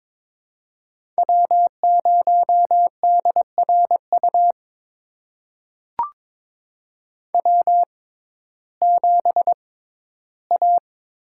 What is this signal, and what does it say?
Morse code, heard as a keyed tone, about 22 wpm, spaced as for 13 wpm, 700 Hz. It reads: W0DRU W7A
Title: USA Call Signs - 5 Characters - 6x Character Spacing - 22wpm